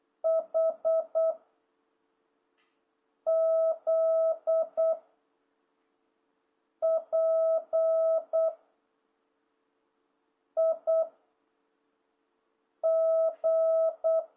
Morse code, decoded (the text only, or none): H Z P I G